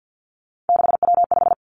Morse code, heard as a keyed tone, about 50 wpm, 700 Hz. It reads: 6W5
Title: Three Challenging Letter-Number Combinations 50wpm